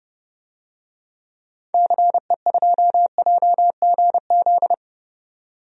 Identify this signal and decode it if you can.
Morse code, heard as a keyed tone, about 30 wpm, 700 Hz. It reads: CE2JGZ
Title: DXCC-100 Call Signs - 6 Characters - 30wpm